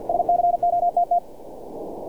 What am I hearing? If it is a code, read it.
Morse code, heard as a keyed tone, about 50 wpm, 700 Hz. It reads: SCOTT
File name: archived / CRSnet